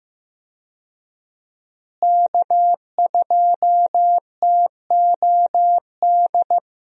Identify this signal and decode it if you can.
Morse code, heard as a keyed tone, about 15 wpm, 700 Hz. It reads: K2TOD
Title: USA Call Signs - 5 Characters - Repeated 2x - 15wpm